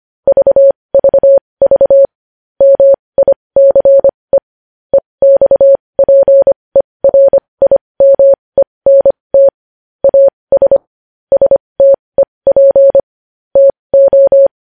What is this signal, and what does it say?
Morse code, heard as a keyed tone, about 25 wpm, spaced as for 20 wpm, 550 Hz. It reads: VVV MICE EXPERIMENT AS STEP TO
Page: News Headlines in Morse Code at 20 WPM